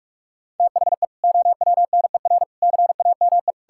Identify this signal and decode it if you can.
Morse code, heard as a keyed tone, about 46 wpm, 700 Hz. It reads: THE OWNER CAME
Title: Sentences from Top 1600 Words 45wpm